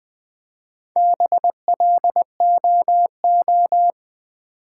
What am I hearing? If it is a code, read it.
Morse code, heard as a keyed tone, about 20 wpm, 700 Hz. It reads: BLOO
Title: Build A Word - 1 Letter At A Time - 5-Letter Words - 20wpm